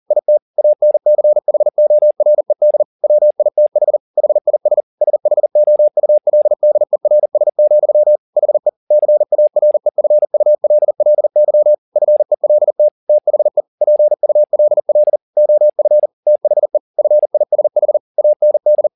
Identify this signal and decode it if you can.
Morse code, heard as a keyed tone, about 40 wpm, 600 Hz. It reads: IT ANCHORED WITH HIS SHOULDERS, HE CAREFULLY FELT THE PULL OF THE FISH AND